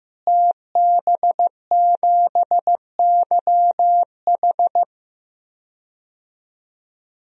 Morse code, heard as a keyed tone, about 15 wpm, 700 Hz. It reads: TB7YH
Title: DXCC-100 Call Signs - 5 Characters - Repeated 3x - 15wpm